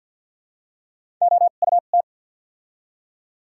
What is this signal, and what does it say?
Morse code, heard as a keyed tone, about 54 wpm, spaced as for 37 wpm, 700 Hz. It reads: OUT
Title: 2x Character Spacing - Top 100 Words - 50wpm